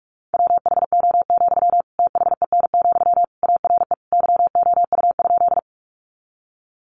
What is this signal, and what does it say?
Morse code, heard as a keyed tone, about 44 wpm, 700 Hz. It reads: WHO, THEN, ARE YOU?
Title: Sentences from Top 100 Words 45wpm